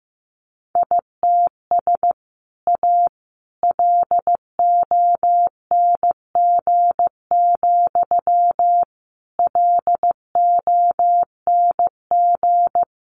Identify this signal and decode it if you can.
Morse code, heard as a keyed tone, about 15 wpm, 700 Hz. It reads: ITS A LONG, LONG
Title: Sentences from Top 600 Words 15wpm